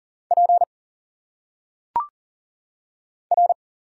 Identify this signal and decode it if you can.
Morse code, heard as a keyed tone, about 40 wpm, 700 Hz. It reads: P R
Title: Single Letters - Phonetic Pronunciation 40wpm